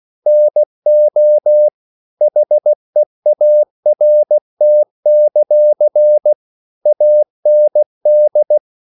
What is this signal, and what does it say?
Morse code, heard as a keyed tone, about 16 wpm, 600 Hz. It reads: NO HEART; AND
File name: CW / Practice / EnglishQuotations0216